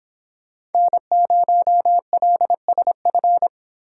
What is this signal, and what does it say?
Morse code, heard as a keyed tone, about 26 wpm, 700 Hz. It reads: N0LSF